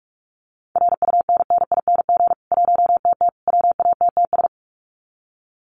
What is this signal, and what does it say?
Morse code, heard as a keyed tone, about 44 wpm, 700 Hz. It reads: RUNNING 1TT WATTS
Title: Common QSO Phrases 45wpm